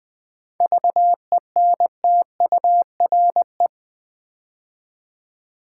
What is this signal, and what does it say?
Morse code, heard as a keyed tone, about 21 wpm, 700 Hz. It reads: VENTURE
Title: 7 Letter Words - First 3 Letters Given - 20wpm